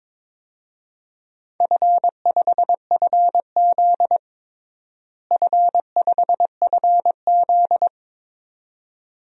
Morse code, heard as a keyed tone, about 22 wpm, 700 Hz. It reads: F5FZ F5FZ
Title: DXCC-100 Call Signs - 4 Characters - Repeated 2x - 22wpm